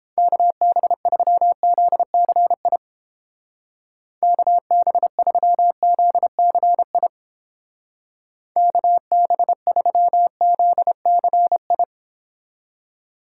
Morse code, heard as a keyed tone, about 29 wpm, 700 Hz. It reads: KB3ZCI KB3ZCI KB3ZCI